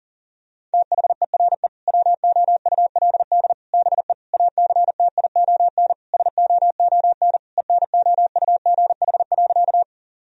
Morse code, heard as a keyed tone, about 40 wpm, 690 Hz. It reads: THERE WOULD BE ACTION SOON ENOUGH.
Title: Sentences from Top 1000 Words - Review Part 4 - 40wpm